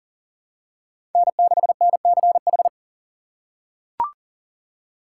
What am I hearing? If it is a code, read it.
Morse code, heard as a keyed tone, about 39 wpm, 700 Hz. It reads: N6NCH
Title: USA Call Signs - 6x Character Spacing - 40wpm